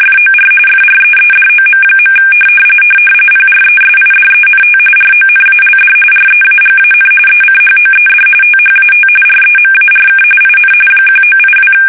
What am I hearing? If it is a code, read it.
RTTY, about 45.45 baud. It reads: FOR FREEDOM. PRESIDENT RONALD REAGAN ECHOED THAT REMARK IN 1982
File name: fsk850 / battleshipnj